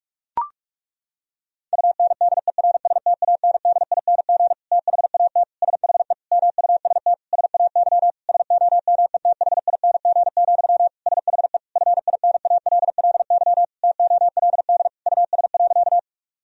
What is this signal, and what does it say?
Morse code, heard as a keyed tone, about 41 wpm, 700 Hz. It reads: UNDERSTANDING THAT SHE MUST SAY SOMETHING, SHE FINALLY TOLD US.
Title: Sentences from Top 1000 Words - Review Part 3 - 45wpm